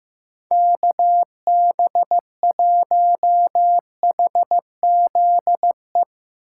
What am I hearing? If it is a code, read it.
Morse code, heard as a keyed tone, about 15 wpm, 700 Hz. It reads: KB1HZE
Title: USA Call Signs - 6 Characters - 2x Character Spacing - 15wpm